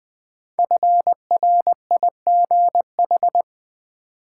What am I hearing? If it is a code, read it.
Morse code, heard as a keyed tone, about 20 wpm, 700 Hz. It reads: FRIGH
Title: Build A Word - 5 Letters At A Time - 10-Letter Words - 20wpm